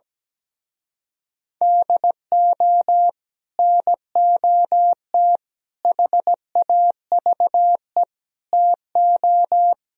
Morse code, heard as a keyed tone, about 17 wpm, 700 Hz. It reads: DO NOT HAVE TO
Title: Speed-Racing - Sets of 5 Words 17wpm